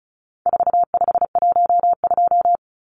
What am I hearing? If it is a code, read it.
Morse code, heard as a keyed tone, about 34 wpm, 700 Hz. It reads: 4512
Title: Numbers - 4-Digit 35wpm